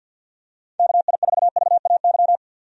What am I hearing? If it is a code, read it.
Morse code, heard as a keyed tone, about 46 wpm, 700 Hz. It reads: KI4VAY
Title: USA Call Signs - 6 Characters - 50wpm